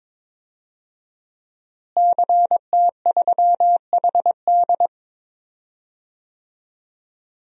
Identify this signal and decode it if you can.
Morse code, heard as a keyed tone, about 22 wpm, 700 Hz. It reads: CT3HD
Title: DXCC-100 Call Signs - 5 Characters - Repeated 2x - 22wpm